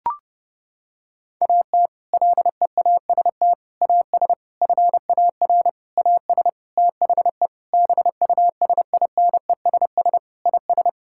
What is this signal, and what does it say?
Morse code, heard as a keyed tone, about 30 wpm, 700 Hz. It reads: AT LEAST AS FAR AS THE BUSINESS IS
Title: Sentences from Top 1000 Words - Review Part 6 - 30wpm